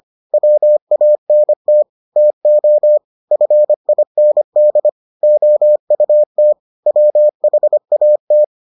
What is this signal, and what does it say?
Morse code, heard as a keyed tone, about 25 wpm, 600 Hz. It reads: WANT TO FIND OUT WHAT